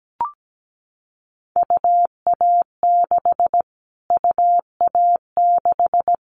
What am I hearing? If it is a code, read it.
Morse code, heard as a keyed tone, about 17 wpm, 700 Hz. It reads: UA6 UA6
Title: Callsign Prefixes for Top 25 DXCC - Repeated 3x - 17wpm